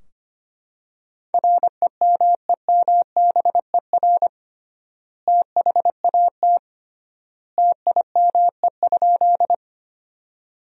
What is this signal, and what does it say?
Morse code, heard as a keyed tone, about 25 wpm, 700 Hz. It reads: REMEMBER THAT TIME?
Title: 3x Word-Spacing - Sentences from Top 400 Words - 25wpm